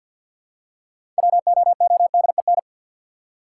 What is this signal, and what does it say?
Morse code, heard as a keyed tone, about 52 wpm, 700 Hz. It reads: WOODEN
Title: Six Letter Words - 50wpm